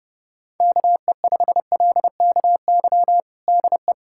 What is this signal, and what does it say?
Morse code, heard as a keyed tone, about 30 wpm, 700 Hz. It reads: KE5LKY DE
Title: USA Callsign DE Callsign - 6 Characters - 30wpm